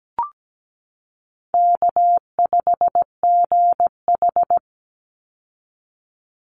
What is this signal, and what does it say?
Morse code, heard as a keyed tone, about 17 wpm, 700 Hz. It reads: K5GH
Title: Easy Call Signs - 17wpm